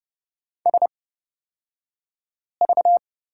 Morse code, heard as a keyed tone, about 29 wpm, spaced as for 5 wpm, 700 Hz. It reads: SV
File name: Lesson 36d - BT - Two Characters 30wpm